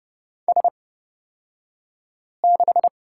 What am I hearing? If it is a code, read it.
Morse code, heard as a keyed tone, about 30 wpm, spaced as for 5 wpm, 700 Hz. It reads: S6
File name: Lesson 37d - Letter x - Two Characters 30wpm